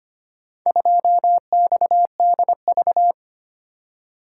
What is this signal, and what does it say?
Morse code, heard as a keyed tone, about 25 wpm, 700 Hz. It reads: 2XDV
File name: practice / 25wpm / Four Letter-Number Combinations - 25wpm